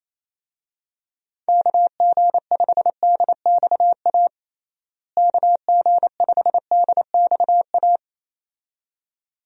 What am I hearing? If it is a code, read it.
Morse code, heard as a keyed tone, about 28 wpm, 700 Hz. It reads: KG5DXA KG5DXA
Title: USA Call Signs - 6 Characters - Repeated 2x - 28wpm